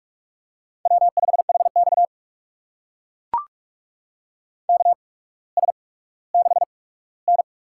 Morse code, heard as a keyed tone, about 44 wpm, spaced as for 19 wpm, 700 Hz. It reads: W5HX KS6N
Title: USA Call Signs - 8x Character Spacing - 45wpm